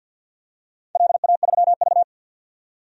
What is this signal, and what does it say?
Morse code, heard as a keyed tone, about 51 wpm, 700 Hz. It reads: LA3V
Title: DXCC-100 Call Signs - 4 Characters - 50wpm